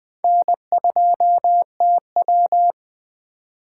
Morse code, heard as a keyed tone, about 20 wpm, 700 Hz. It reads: N2TW
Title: Speed Racing - USA 1x2 Call Signs 17wpm